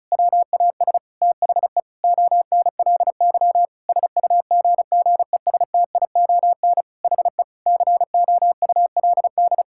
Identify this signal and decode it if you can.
Morse code, heard as a keyed tone, about 36 wpm, 700 Hz. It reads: WAS THE ONLY SUGGESTION HE COULD
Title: Sentences from Top 1600 Words 35wpm